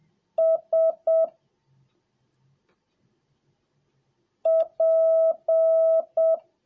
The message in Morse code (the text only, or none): S P